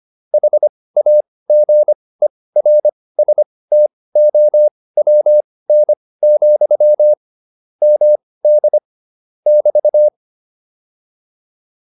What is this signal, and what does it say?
Morse code, heard as a keyed tone, about 25 wpm, spaced as for 18 wpm, 600 Hz. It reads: HAGERSTOWN, MD <BT>